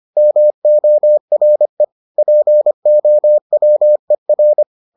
Morse code, heard as a keyed tone, about 25 wpm, 600 Hz. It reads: MORE POWER